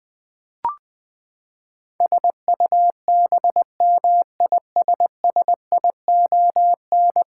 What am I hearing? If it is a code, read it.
Morse code, heard as a keyed tone, about 21 wpm, 700 Hz. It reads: SUBMISSION
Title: Ten Letter Words - 20wpm